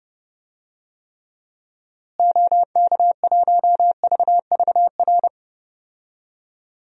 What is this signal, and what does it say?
Morse code, heard as a keyed tone, about 30 wpm, 700 Hz. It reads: OK1VVR
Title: DXCC-100 Call Signs - 6 Characters - Repeated 3x - 30wpm